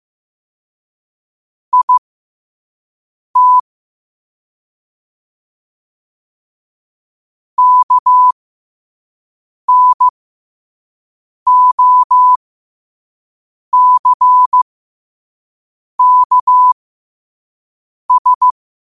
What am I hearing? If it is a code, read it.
Morse code, heard as a keyed tone, about 15 wpm, spaced as for 5 wpm, 1000 Hz. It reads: IT KNOCKS
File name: 5wpm